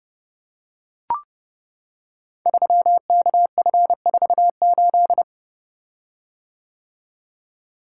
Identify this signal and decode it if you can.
Morse code, heard as a keyed tone, about 30 wpm, 700 Hz. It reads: 3KF48
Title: Code Groups - 5 Letter:Numbers - 30wpm